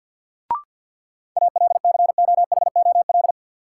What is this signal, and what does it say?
Morse code, heard as a keyed tone, about 47 wpm, 700 Hz. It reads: ALCOHOL
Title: Sentences from Top 1400 Words - New Word Drill Alt - 50wpm